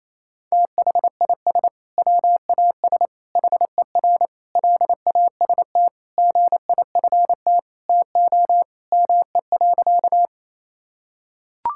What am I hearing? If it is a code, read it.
Morse code, heard as a keyed tone, about 27 wpm, 700 Hz. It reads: THIS WAS HER LAST GIFT TO ME.
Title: Sentences from Top 1000 Words - Review Part 6 - 28wpm